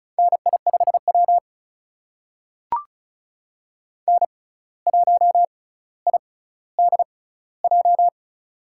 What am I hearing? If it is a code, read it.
Morse code, heard as a keyed tone, about 34 wpm, spaced as for 17 wpm, 700 Hz. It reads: NI5W N1IDJ
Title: USA Call Signs - 6x Character Spacing - 35wpm